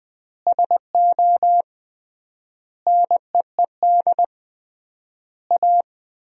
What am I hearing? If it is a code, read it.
Morse code, heard as a keyed tone, about 20 wpm, 700 Hz. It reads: SO NEED A
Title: 3x Word-Spacing - Sentences from Top 200 Words - 20wpm